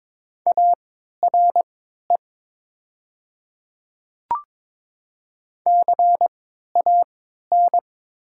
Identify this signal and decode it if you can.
Morse code, heard as a keyed tone, about 22 wpm, spaced as for 12 wpm, 700 Hz. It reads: ARE CAN
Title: 3x Character Spacing - Top 300 Words - 22wpm